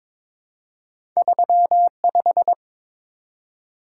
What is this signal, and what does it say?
Morse code, heard as a keyed tone, about 22 wpm, 690 Hz. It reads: 35
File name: practice / 22wpm / Two Letter-Number Combinations - 22wpm